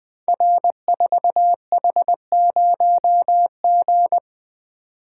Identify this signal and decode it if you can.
Morse code, heard as a keyed tone, about 20 wpm, 700 Hz. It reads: R4H0G
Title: Code Groups - 5 Letter:Numbers - 20wpm